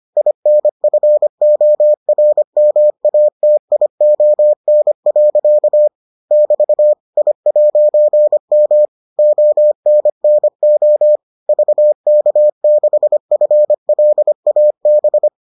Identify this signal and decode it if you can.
Morse code, heard as a keyed tone, about 25 wpm, 600 Hz. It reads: INFORMATION. <BT> I'M ONNO VK6FLAB